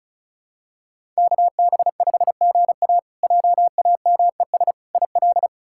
Morse code, heard as a keyed tone, about 36 wpm, 700 Hz. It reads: KB5GA JAMES IL
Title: K1USN SST QSO Practice - 35wpm